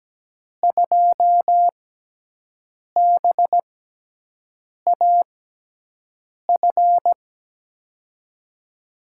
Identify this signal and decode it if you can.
Morse code, heard as a keyed tone, about 17 wpm, spaced as for 6 wpm, 700 Hz. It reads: 2BAF
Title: USA Call Signs - 5 Characters - 6x Character Spacing - 17wpm